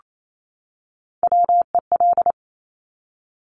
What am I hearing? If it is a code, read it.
Morse code, heard as a keyed tone, about 28 wpm, 700 Hz. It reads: WEL
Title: Top Three Letter Combinations in Words 28wpm